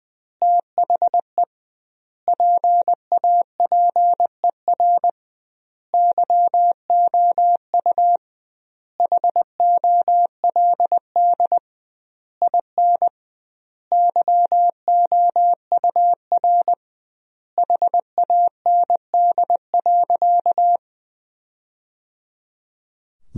Morse code, heard as a keyed tone, about 19 wpm, 700 Hz. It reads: THE PAPER YOU HOLD IN YOUR HAND.
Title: 2x Word-Spacing - Sentences from Top 400 Words - 20wpm